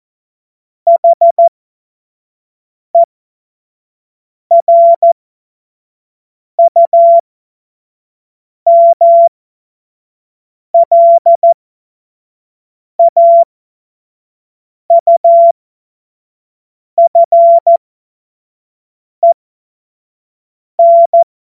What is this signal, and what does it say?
Morse code, heard as a keyed tone, about 14 wpm, spaced as for 5 wpm, 680 Hz. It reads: HERUMLAUFEN
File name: Morsekurs / Witze / Witz 22. November 2022 25 F70